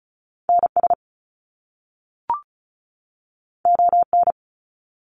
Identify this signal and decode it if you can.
Morse code, heard as a keyed tone, about 36 wpm, 700 Hz. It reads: NS ON
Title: Canadian Provinces and Territories 35wpm